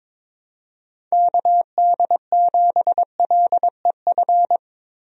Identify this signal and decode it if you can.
Morse code, heard as a keyed tone, about 22 wpm, 700 Hz. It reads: KD7LEF
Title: USA Call Signs - 6 Characters - Repeated 2x - 22wpm